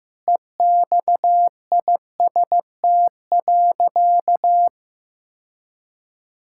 Morse code, heard as a keyed tone, about 15 wpm, 700 Hz. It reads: EXIST.
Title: Sentences from Top 1000 Words - Review Part 5 - 15wpm